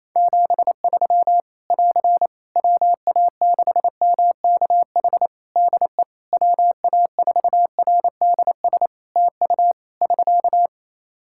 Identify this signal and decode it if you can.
Morse code, heard as a keyed tone, about 27 wpm, 700 Hz. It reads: 73 <AR> WA6MKH DE WA4RDS TU <SK>
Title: Practice QSO - Exchange 4 - Ending - 28wpm